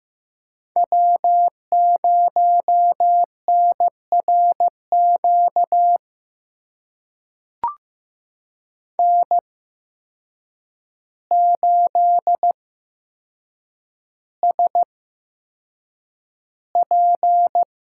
Morse code, heard as a keyed tone, about 15 wpm, spaced as for 7 wpm, 700 Hz. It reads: W0NRQ N8SP
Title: USA Call Signs - 5 Characters - 8x Character Spacing - 15wpm